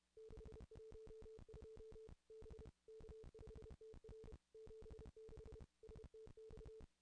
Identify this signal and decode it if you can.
Morse code, heard as a keyed tone, about 31 wpm, 440 Hz. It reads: 612 B K5TR 76 STX
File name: contests / ss-cw-k5tr-op-n5rz -0200